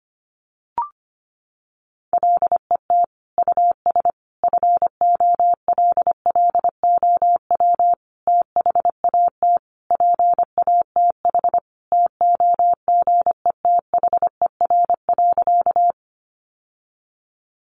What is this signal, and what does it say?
Morse code, heard as a keyed tone, about 25 wpm, 700 Hz. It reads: LET US FOLLOW THAT PATH TOGETHER.
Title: Sentences from Top 1400 Words 25wpm